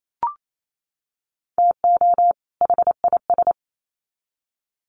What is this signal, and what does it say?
Morse code, heard as a keyed tone, about 28 wpm, 700 Hz. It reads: TO HIS
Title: Sets of 2 Words 28wpm